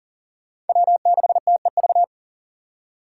Morse code, heard as a keyed tone, about 47 wpm, 700 Hz. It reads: W6TEV
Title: Speed Racing - USA 1x3 Call Signs 40wpm